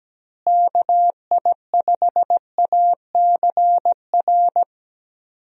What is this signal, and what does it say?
Morse code, heard as a keyed tone, about 17 wpm, 700 Hz. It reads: KI5ACR
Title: USA Call Signs - 6 Characters - Repeated 2x - 17wpm